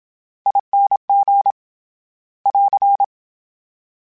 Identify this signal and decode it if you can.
Morse code, heard as a keyed tone, about 27 wpm, 800 Hz. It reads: ING <AR>
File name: Special INGs / Special INGs-mp3s / VerySlowToFaster Ing @25